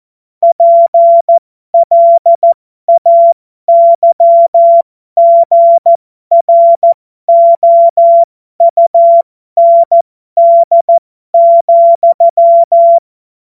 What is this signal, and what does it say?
Morse code, heard as a keyed tone, about 14 wpm, spaced as for 12 wpm, 680 Hz. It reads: PLAYGROUND,
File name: Morsekurs / Witze